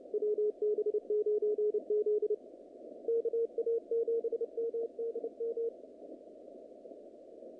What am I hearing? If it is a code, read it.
Morse code, heard as a keyed tone, about 29 wpm, 430 Hz, with radio noise. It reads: WB9Z KA7MDM
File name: contests / naqp-cw-k5tr-op-k5pi- -2200